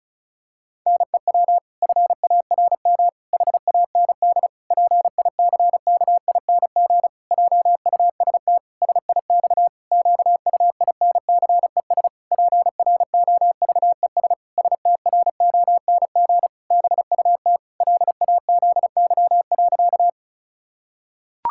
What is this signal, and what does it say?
Morse code, heard as a keyed tone, about 36 wpm, 700 Hz. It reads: NEW FARM HAND PICKING JUST SIX QUINCES PROVES STRONG BUT LAZY.
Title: Pangram Challenge v2 - 35wpm